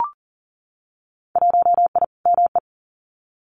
Morse code, heard as a keyed tone, about 40 wpm, 700 Hz. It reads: 1I ME
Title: Winter Field Day Exchanges 40wpm